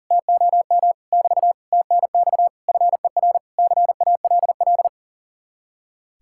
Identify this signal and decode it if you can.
Morse code, heard as a keyed tone, about 53 wpm, 700 Hz. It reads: TOM <BT> TNX FER CALL